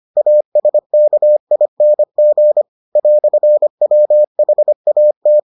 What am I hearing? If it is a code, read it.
Morse code, heard as a keyed tone, about 25 wpm, 600 Hz. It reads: ASKING "WHAT